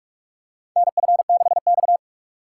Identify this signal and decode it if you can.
Morse code, heard as a keyed tone, about 44 wpm, 700 Hz. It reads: NF6X
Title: USA Call Signs - 2x Character Spacing - 45wpm